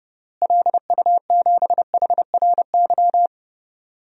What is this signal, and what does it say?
Morse code, heard as a keyed tone, about 30 wpm, 700 Hz. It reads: LU7HRY
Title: DXCC-100 Call Signs - 6 Characters - Repeated 3x - 30wpm